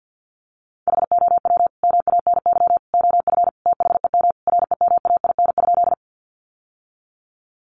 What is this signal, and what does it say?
Morse code, heard as a keyed tone, about 50 wpm, 700 Hz. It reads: HOW MANY OF THEM REMAIN?
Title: Sentences from Top 1200 Words 50wpm